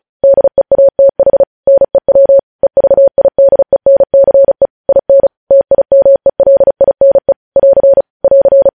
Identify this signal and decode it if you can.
Morse code, heard as a keyed tone, about 36 wpm, 550 Hz. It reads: DEATH NEW EVIDENCE IN TIMELINE <AR> <AR>
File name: podcast / News headlines-1x-35x35